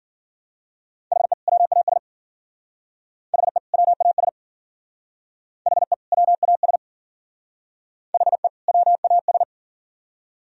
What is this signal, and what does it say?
Morse code, heard as a keyed tone, about 50 wpm, 700 Hz. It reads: HE WAS HE WAS HE WAS HE WAS